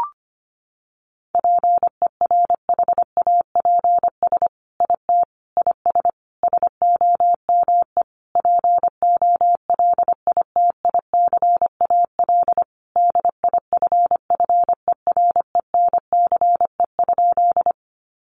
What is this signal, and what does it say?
Morse code, heard as a keyed tone, about 25 wpm, 700 Hz. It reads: PERHAPS IT IS SOME POLITICAL DIFFERENCE?
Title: Sentences from Top 800 Words 25wpm